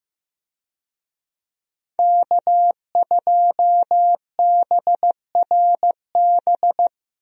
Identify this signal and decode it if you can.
Morse code, heard as a keyed tone, about 15 wpm, 700 Hz. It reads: K2BRB
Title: USA Call Signs - 5 Characters - Repeated 2x - 15wpm